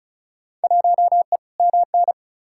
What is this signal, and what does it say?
Morse code, heard as a keyed tone, about 36 wpm, 700 Hz. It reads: 1E MN
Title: Field Day Exchanges - 35wpm